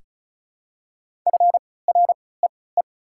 Morse code, heard as a keyed tone, about 34 wpm, spaced as for 20 wpm, 690 Hz. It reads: FREE
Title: 3x Character Spacing - Top 500 Words - 35wpm